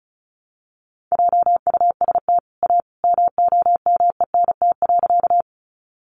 Morse code, heard as a keyed tone, about 34 wpm, 700 Hz. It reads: JUST A MOMENT.
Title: Sentences from Top 1000 Words 35wpm